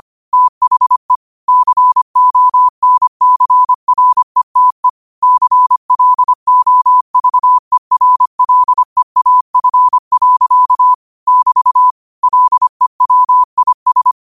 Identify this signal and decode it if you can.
Morse code, heard as a keyed tone, about 25 wpm, 1000 Hz. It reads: THE CONCRETE CLOVERLEAF. <BT> LEWIS
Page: New quotes every day in morse code at 25 Words per minute.